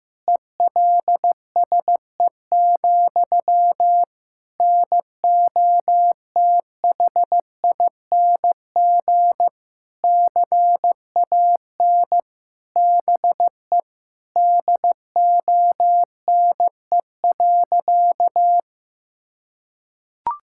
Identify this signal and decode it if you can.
Morse code, heard as a keyed tone, about 15 wpm, 700 Hz. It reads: ELSE, NOTHING CAN BE DONE.
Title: Sentences from Top 1600 Words 15wpm